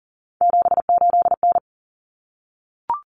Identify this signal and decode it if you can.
Morse code, heard as a keyed tone, about 40 wpm, 700 Hz. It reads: 78N
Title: Common Cut Numbers - 3-Digit 40wpm